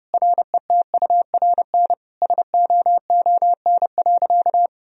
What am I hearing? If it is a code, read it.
Morse code, heard as a keyed tone, about 30 wpm, 700 Hz. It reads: RETURN SOON.
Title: Sentences from Top 1000 Words - Review Part 2 - 30wpm